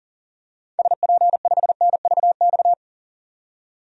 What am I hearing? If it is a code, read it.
Morse code, heard as a keyed tone, about 38 wpm, 700 Hz. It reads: SP5NVX
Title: DXCC-100 Call Signs - 6 Characters - 40wpm